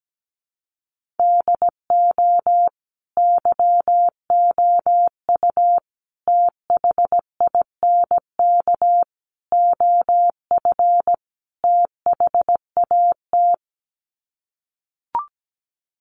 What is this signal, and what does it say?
Morse code, heard as a keyed tone, about 17 wpm, 700 Hz. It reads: DO YOU THINK OF THAT
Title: Sets of 5 Words Encore2 17wpm